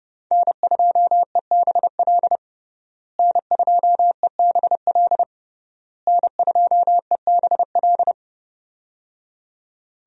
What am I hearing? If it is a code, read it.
Morse code, heard as a keyed tone, about 30 wpm, 700 Hz. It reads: N2EBL N2EBL N2EBL
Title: USA Call Signs - 5 Characters - Repeated 3x - 30wpm